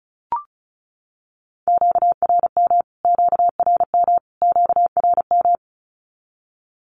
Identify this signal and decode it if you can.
Morse code, heard as a keyed tone, about 34 wpm, 700 Hz. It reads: QRM QRM QRM
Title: Q-Codes Repeated 3x 35wpm